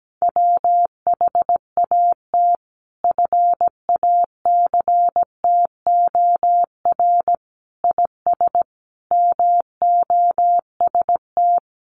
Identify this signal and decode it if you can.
Morse code, heard as a keyed tone, about 17 wpm, 700 Hz. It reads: WHAT FACTOR IS MOST
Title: Sentences from Top 1300 Words 17wpm